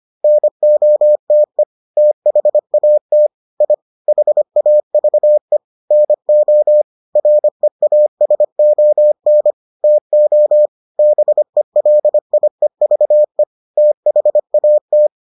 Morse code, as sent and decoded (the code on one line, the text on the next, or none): -. --- - . / - .... .- - / .. / .... .- ...- . / -. --- / .-. .- ... --- -. / - --- / -... .-.. .. ...- . / - .... .- -
NOTE THAT I HAVE NO REASON TO BELIEVE THAT